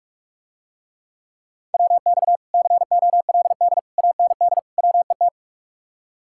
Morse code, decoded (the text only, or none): WX COLD AND WET